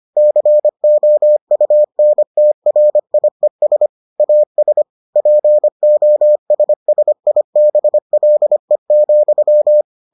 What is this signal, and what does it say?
Morse code, heard as a keyed tone, about 25 wpm, 600 Hz. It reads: COUNTRIES AS POSSIBLE,